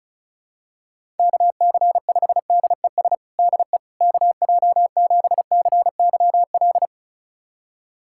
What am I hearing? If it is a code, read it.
Morse code, heard as a keyed tone, about 34 wpm, 700 Hz. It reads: KC5DES DE KJ7CYL
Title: USA Callsign DE Callsign - 6 Characters - 35wpm